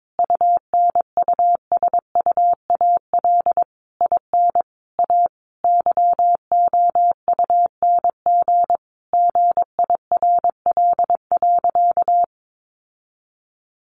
Morse code, heard as a keyed tone, about 22 wpm, 700 Hz. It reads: UNUSUAL IN A YOUNG GIRL.
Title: Sentences from Top 1400 Words 22wpm